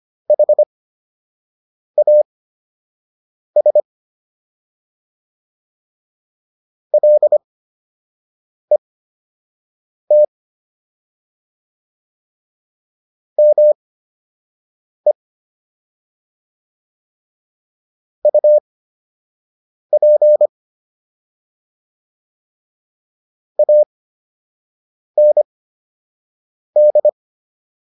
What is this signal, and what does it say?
Morse code, heard as a keyed tone, about 25 wpm, spaced as for 6 wpm, 600 Hz. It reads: HAS LET ME UP AND